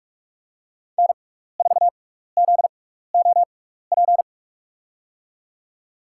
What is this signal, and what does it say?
Morse code, heard as a keyed tone, about 44 wpm, spaced as for 15 wpm, 700 Hz. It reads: N4ZOP
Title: USA Call Signs - 5 Characters - 6x Character Spacing - 45wpm